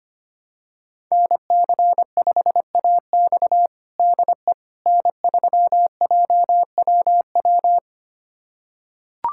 Morse code, heard as a keyed tone, about 25 wpm, 700 Hz. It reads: NC5AX DE N3JWW
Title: USA Callsign DE Callsign - 5 Characters - 25wpm